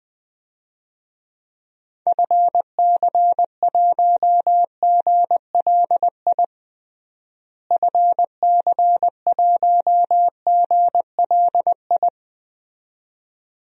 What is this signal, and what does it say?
Morse code, heard as a keyed tone, about 20 wpm, 700 Hz. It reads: FC1GLI FC1GLI
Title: DXCC-100 Call Signs - 6 Characters - Repeated 2x - 20wpm